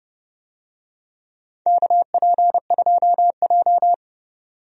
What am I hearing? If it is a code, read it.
Morse code, heard as a keyed tone, about 30 wpm, 700 Hz. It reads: KP2J
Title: DXCC-100 Call Signs - 4 Characters - 30wpm